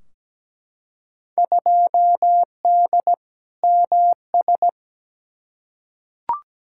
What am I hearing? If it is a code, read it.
Morse code, heard as a keyed tone, about 17 wpm, 700 Hz. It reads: 2D MS
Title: Field Day Exchanges Encore - 17wpm